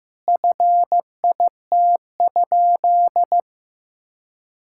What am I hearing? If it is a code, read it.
Morse code, heard as a keyed tone, about 14 wpm, 700 Hz. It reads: FIT?
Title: Sentences from Top 1000 Words - Review Part 5 - 15wpm